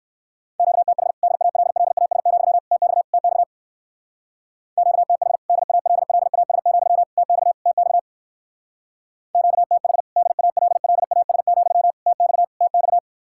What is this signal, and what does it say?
Morse code, heard as a keyed tone, about 56 wpm, 700 Hz. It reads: QTH DALLAS, TX TX QTH DALLAS, TX TX QTH DALLAS, TX TX
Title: Speed-Racing Common QSO Phrases - 45wpm